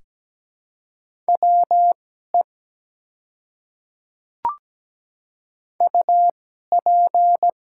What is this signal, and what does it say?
Morse code, heard as a keyed tone, about 17 wpm, spaced as for 12 wpm, 700 Hz. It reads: WE UP
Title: 2x Character Spacing - Top 100 Words - 17wpm